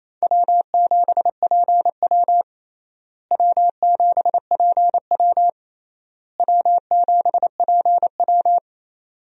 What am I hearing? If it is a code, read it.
Morse code, heard as a keyed tone, about 28 wpm, 700 Hz. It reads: W7PW W7PW W7PW
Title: USA Call Signs - 4 Characters - Repeated 3x - 28wpm